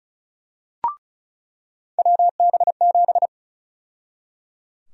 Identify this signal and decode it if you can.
Morse code, heard as a keyed tone, about 34 wpm, 700 Hz. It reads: WB7
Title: USA Call Sign Prefixes - 35wpm